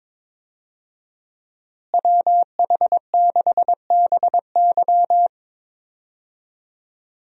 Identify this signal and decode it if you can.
Morse code, heard as a keyed tone, about 22 wpm, 700 Hz. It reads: WH6BY
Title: USA Call Signs - 5 Characters - Repeated 2x - 22wpm